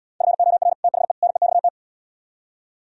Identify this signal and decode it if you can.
Morse code, heard as a keyed tone, about 77 wpm, 700 Hz. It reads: FOR THE NEXT